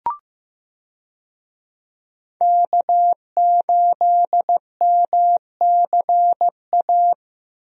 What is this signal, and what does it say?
Morse code, heard as a keyed tone, about 15 wpm, 700 Hz. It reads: K8MCA
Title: USA Call Signs - 5 Characters - Repeated 2x - 15wpm